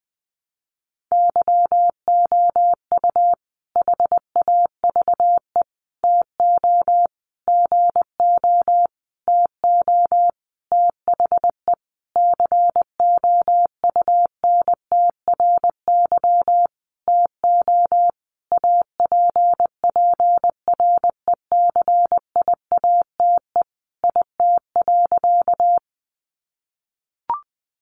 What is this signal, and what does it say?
Morse code, heard as a keyed tone, about 20 wpm, 700 Hz. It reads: YOU HAVE TO GO TO THE COUNTRY TO APPRECIATE IT.
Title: Sentences from Top 1300 Words 20wpm